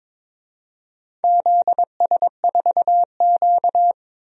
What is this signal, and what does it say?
Morse code, heard as a keyed tone, about 22 wpm, 700 Hz. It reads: ZS4Q
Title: DXCC-100 Call Signs - 4 Characters - 22wpm